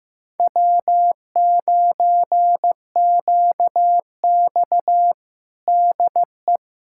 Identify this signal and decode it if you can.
Morse code, heard as a keyed tone, about 15 wpm, 700 Hz. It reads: W9QX DE
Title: USA Callsign DE Callsign - 4 Characters - 15wpm